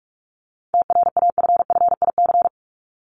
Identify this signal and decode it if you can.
Morse code, heard as a keyed tone, about 44 wpm, 700 Hz. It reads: TRAFFIC
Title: Top 1251-1500 Words 45wpm